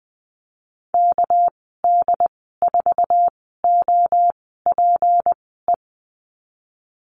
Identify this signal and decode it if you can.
Morse code, heard as a keyed tone, about 20 wpm, spaced as for 15 wpm, 700 Hz. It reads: KD4OPE
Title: 213 Lesson 34g - Number 8 - Call Signs 20wpm